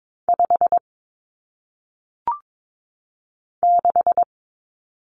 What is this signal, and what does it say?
Morse code, heard as a keyed tone, about 22 wpm, 700 Hz. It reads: 5 6
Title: Numbers - 1-Digit 22wpm